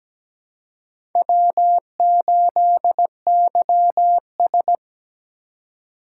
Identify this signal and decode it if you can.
Morse code, heard as a keyed tone, about 17 wpm, 700 Hz. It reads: W8YS
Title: USA Call Signs - 4 Characters - 3x Character Spacing - 17wpm